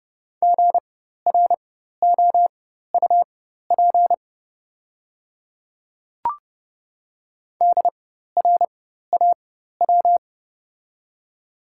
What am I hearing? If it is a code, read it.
Morse code, heard as a keyed tone, about 30 wpm, spaced as for 14 wpm, 700 Hz. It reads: GROUP DRAW
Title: 4x Character Spacing - Top 300 Words - 30wpm